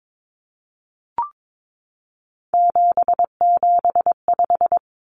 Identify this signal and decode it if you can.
Morse code, heard as a keyed tone, about 22 wpm, 700 Hz. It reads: 775
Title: Numbers - 3-Digit 22wpm